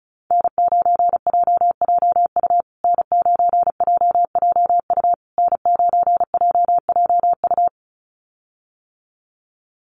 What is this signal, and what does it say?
Morse code, heard as a keyed tone, about 34 wpm, 700 Hz. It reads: N9JJU N9JJU N9JJU
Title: USA Call Signs Repeated 3x 35wpm